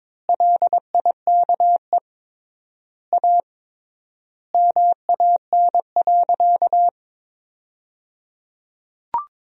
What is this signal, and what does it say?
Morse code, heard as a keyed tone, about 22 wpm, 700 Hz. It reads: LIKE A MAN.
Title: 3x Word-Spacing - Sentences from Top 300 Words - 22wpm